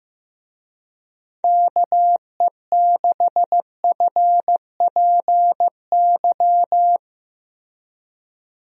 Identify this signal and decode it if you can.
Morse code, heard as a keyed tone, about 15 wpm, 700 Hz. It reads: KE6FPY
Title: USA Call Signs - 6 Characters - Repeated 3x - 15wpm